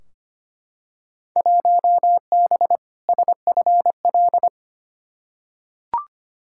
Morse code, heard as a keyed tone, about 25 wpm, 700 Hz. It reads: 1B SFL
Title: Field Day Exchanges Encore - 25wpm